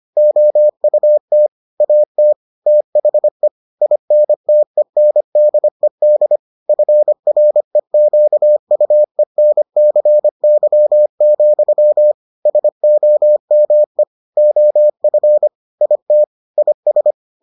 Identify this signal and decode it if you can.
Morse code, heard as a keyed tone, about 25 wpm, 600 Hz. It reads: OUT AT THE INTENDED FREQUENCY, SOME OF IT IS